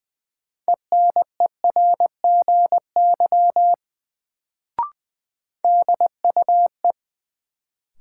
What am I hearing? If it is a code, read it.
Morse code, heard as a keyed tone, about 20 wpm, 700 Hz. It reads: ENERGY DUE
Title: Sentences from Top 900 Words - New Word Drill - 20wpm